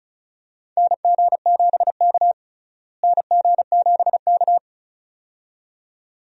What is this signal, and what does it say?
Morse code, heard as a keyed tone, about 36 wpm, 700 Hz. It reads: NG7K NG7K
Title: USA Call Signs - 4 Characters - Repeated 2x - 35wpm